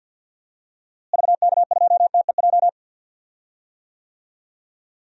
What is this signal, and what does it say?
Morse code, heard as a keyed tone, about 48 wpm, 700 Hz. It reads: VK2TEJ
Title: DXCC-100 Call Signs - 6 Characters - Repeated 2x - 50wpm